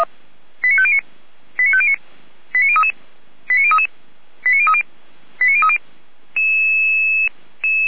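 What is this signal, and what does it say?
Five-tone selective call (ZVEI), heard as quick runs of five tones, tones of about 70 ms. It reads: 89509 89509 89033 89033 89030 89030
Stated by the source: ZVEI five-tone selective call received on SDR, from feuerwehr-orlamuende.org (fire brigade Orlamünde).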